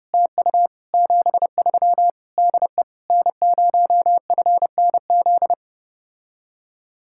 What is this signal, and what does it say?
Morse code, heard as a keyed tone, about 29 wpm, 700 Hz. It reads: TU 73 DE N0FNZ